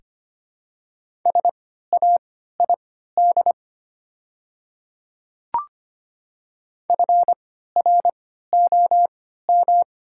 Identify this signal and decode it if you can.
Morse code, heard as a keyed tone, about 25 wpm, spaced as for 14 wpm, 690 Hz. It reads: SAID FROM
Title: 3x Character Spacing - Top 100 Words - 25wpm